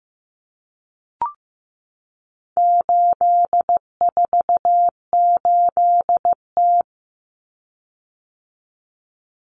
Morse code, heard as a keyed tone, about 15 wpm, 700 Hz. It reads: 848T
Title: Common Cut Numbers - 4-Digit 15wpm